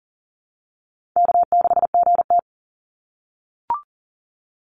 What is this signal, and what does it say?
Morse code, heard as a keyed tone, about 40 wpm, 700 Hz. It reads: K6GT
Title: Easy Call Signs - 40wpm